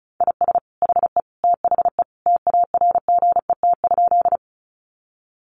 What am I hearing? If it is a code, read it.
Morse code, heard as a keyed tone, about 34 wpm, 700 Hz. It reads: IS HE THE TARGET?
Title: Sentences from Top 1200 Words 35wpm